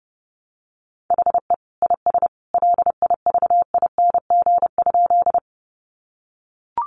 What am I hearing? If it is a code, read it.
Morse code, heard as a keyed tone, about 30 wpm, 700 Hz. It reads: HE IS LIVING?
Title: Sentences from Top 1100 Words 30wpm